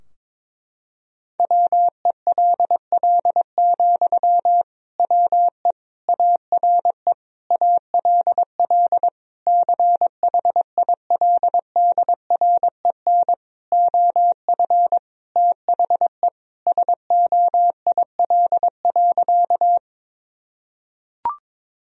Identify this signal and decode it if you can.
Morse code, heard as a keyed tone, about 22 wpm, 700 Hz. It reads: WELL, WE ARE ALL CHILDREN OF THE SOIL.
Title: Sentences from Top 1000 Words - Review Part 4 - 22wpm